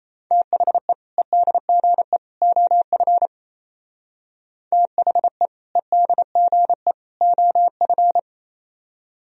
Morse code, heard as a keyed tone, about 34 wpm, 700 Hz. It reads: THE EDGE OF THE EDGE OF
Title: Speed-Racing - Sets of 3 Words 28wpm